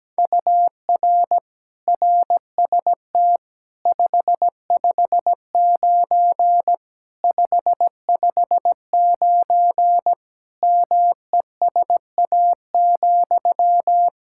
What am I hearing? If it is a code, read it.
Morse code, heard as a keyed tone, about 17 wpm, 700 Hz. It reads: UR RST 559 559 MESA,
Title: Practice QSO - Exchange 1 - Beginning - 17wpm